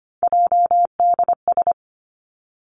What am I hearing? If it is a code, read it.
Morse code, heard as a keyed tone, about 25 wpm, 700 Hz. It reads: JDS
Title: Three Challenging Letter-Number Combinations 25wpm